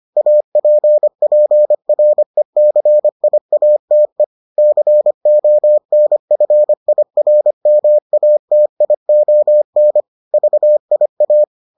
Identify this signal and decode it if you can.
Morse code, heard as a keyed tone, about 25 wpm, 600 Hz. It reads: APPRECIATE CONFIRMATION VIA